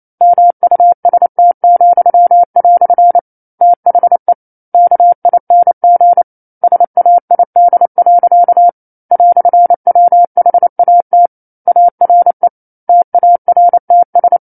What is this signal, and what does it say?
Morse code, heard as a keyed tone, about 28 wpm, 700 Hz. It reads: MUST," THE KING SAID. "WHAT ARE TARTS
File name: 09-Chapter-09 28 WPM